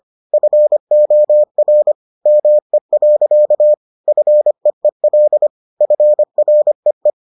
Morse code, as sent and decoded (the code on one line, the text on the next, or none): ..-. --- .-. / -- . .-.-.- / ..-. . .-.. / ..-. .-. .
FOR ME. FEEL FREE